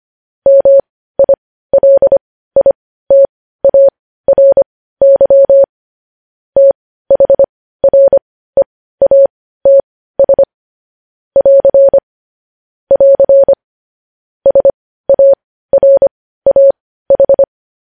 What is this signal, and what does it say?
Morse code, heard as a keyed tone, about 25 wpm, spaced as for 15 wpm, 550 Hz. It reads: MILITARY THREATS <AR> <AR> SARAH
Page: news-headlines-in-morse-code-at-15-wpm